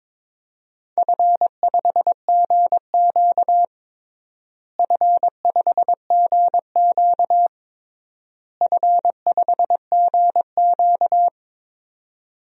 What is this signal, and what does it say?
Morse code, heard as a keyed tone, about 22 wpm, 700 Hz. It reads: F5GQ F5GQ F5GQ
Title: DXCC-100 Call Signs - 4 Characters - Repeated 3x - 22wpm